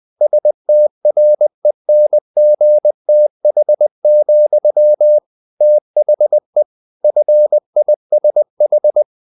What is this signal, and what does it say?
Morse code, heard as a keyed tone, about 20 wpm, 600 Hz. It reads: STRENGTH, THE FISH
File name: the old man and the sea part 2 20wpm